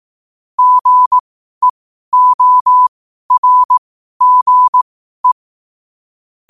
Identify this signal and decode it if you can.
Morse code, heard as a keyed tone, about 18 wpm, spaced as for 12 wpm, 1000 Hz. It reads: GEORGE
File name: qotdfiles / 12wpm